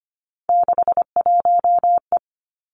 Morse code, heard as a keyed tone, about 25 wpm, 700 Hz. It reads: 61E
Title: Three Challenging Letter-Number Combinations 25wpm